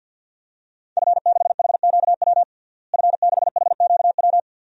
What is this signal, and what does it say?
Morse code, heard as a keyed tone, about 52 wpm, 700 Hz. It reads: F6HQW F6HQW
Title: DXCC-100 Call Signs - 5 Characters - Repeated 2x - 50wpm